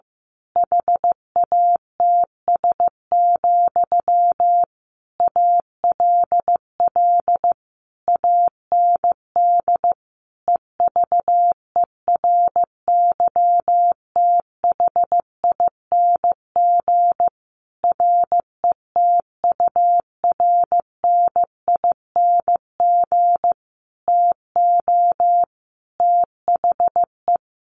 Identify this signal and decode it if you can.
Morse code, heard as a keyed tone, about 15 wpm, 700 Hz. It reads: HATS, ALL AND EVERYTHING RETURNING TO THE
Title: A Sound Of Thunder 15wpm